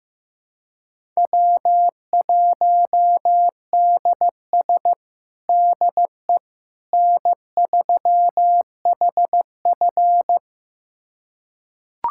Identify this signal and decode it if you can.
Morse code, heard as a keyed tone, about 15 wpm, 700 Hz. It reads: W1DS DE N3HF
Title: USA Callsign DE Callsign - 4 Characters - 15wpm